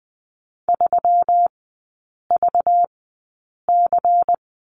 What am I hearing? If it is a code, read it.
Morse code, heard as a keyed tone, about 20 wpm, spaced as for 8 wpm, 690 Hz. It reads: 3VC
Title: 198 Lesson 32e - Letter k - Three Characters 20wpm